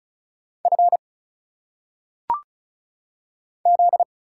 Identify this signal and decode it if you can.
Morse code, heard as a keyed tone, about 34 wpm, 700 Hz. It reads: F Z
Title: Single Letters - Phonetic Pronunciation 35wpm